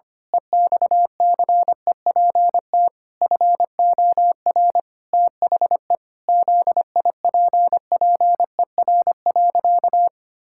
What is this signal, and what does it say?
Morse code, heard as a keyed tone, about 25 wpm, 700 Hz. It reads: EXCEPT FOR THE ZIPPER.
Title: Pangram Challenge v2 - 25wpm